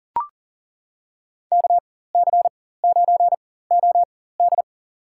Morse code, heard as a keyed tone, about 40 wpm, spaced as for 19 wpm, 700 Hz. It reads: KC9OD
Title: USA Call Signs - 4x Character Spacing - 40wpm